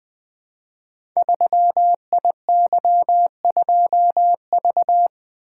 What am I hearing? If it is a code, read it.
Morse code, heard as a keyed tone, about 20 wpm, 700 Hz. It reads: 3IY2V
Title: Code Groups - 5 Letter:Numbers - 20wpm